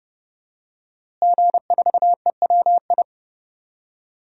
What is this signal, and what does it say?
Morse code, heard as a keyed tone, about 30 wpm, 700 Hz. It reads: G4EWI
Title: DXCC-100 Call Signs - 5 Characters - Repeated 2x - 30wpm